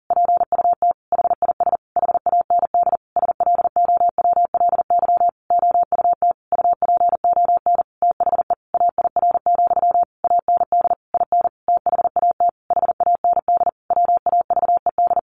Morse code, heard as a keyed tone, about 40 wpm, 700 Hz. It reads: PUT HIS HAND SLOWLY OUT UPON THE AIR, AND IN THAT HAND WAVED
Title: A Sound Of Thunder 40wpm